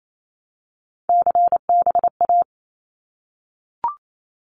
Morse code, heard as a keyed tone, about 28 wpm, 700 Hz. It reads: CBA
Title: Abbreviations Repeated 3x - 28wpm